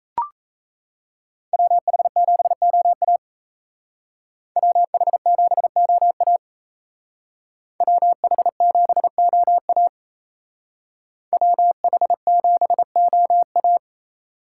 Morse code, heard as a keyed tone, about 36 wpm, 700 Hz. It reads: WH7OA WH7OA WH7OA WH7OA